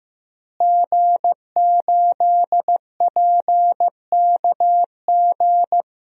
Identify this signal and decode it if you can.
Morse code, heard as a keyed tone, about 15 wpm, 700 Hz. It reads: G8PKG
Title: DXCC-100 Call Signs - 5 Characters - Repeated 3x - 15wpm